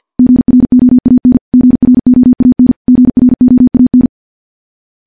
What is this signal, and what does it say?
Morse code, heard as a keyed tone, about 50 wpm, 250 Hz. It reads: GROAN GROAN GROAN
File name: morse / morse-training-50WPM